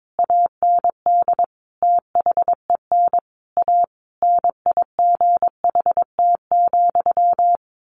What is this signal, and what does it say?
Morse code, heard as a keyed tone, about 22 wpm, 700 Hz. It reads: AND THEN A NIGHT,
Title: A Sound Of Thunder 22wpm